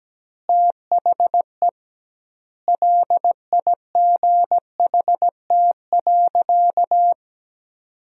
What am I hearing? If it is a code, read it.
Morse code, heard as a keyed tone, about 17 wpm, 700 Hz. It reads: THE LIGHT.
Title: 2x Word-Spacing - Sentences from Top 400 Words - 17wpm